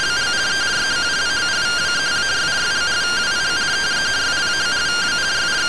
Two-tone teleprinter signal, about 170 Hz shift.